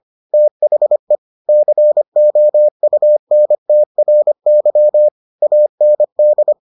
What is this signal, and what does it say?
Morse code, heard as a keyed tone, about 25 wpm, 600 Hz. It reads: THE COUNTRY AND